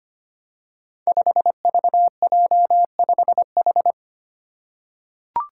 Morse code, heard as a keyed tone, about 25 wpm, 700 Hz. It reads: 5VJ5H
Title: Five Challenging Letter-Number Combinations 25wpm